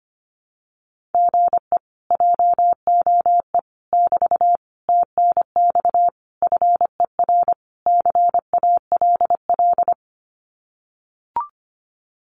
Morse code, heard as a keyed tone, about 25 wpm, 700 Hz. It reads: GE JOE <BT> TNX FER CALL
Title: Common QSO Phrases Encore 25wpm